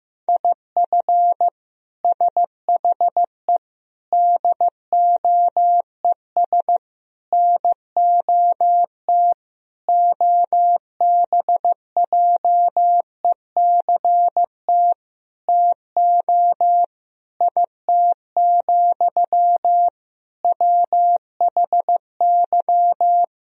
Morse code, heard as a keyed tone, about 15 wpm, 700 Hz. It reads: IF SHE DOES NOT OBJECT TO IT, WHY
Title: Sentences from Top 1000 Words - Review Part 3 - 15wpm